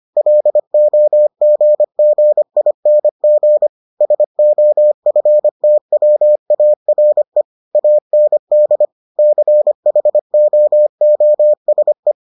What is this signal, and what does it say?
Morse code, heard as a keyed tone, about 25 wpm, 600 Hz. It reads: LOGGING SOFTWARE AND CHOOSE